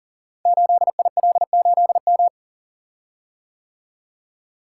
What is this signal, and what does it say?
Morse code, heard as a keyed tone, about 41 wpm, 700 Hz. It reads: 8IL8M
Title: Code Groups - 5 Letter:Numbers - 40wpm